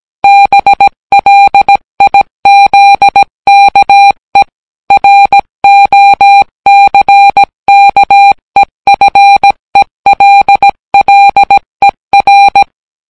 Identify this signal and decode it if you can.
Morse code, heard as a keyed tone, about 17 wpm, 800 Hz. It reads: BLIZKE ROCKEFELLER